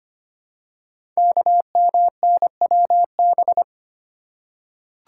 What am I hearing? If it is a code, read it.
Morse code, heard as a keyed tone, about 25 wpm, 700 Hz. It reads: KMNWB
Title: Code Groups - 5 Letters - 25wpm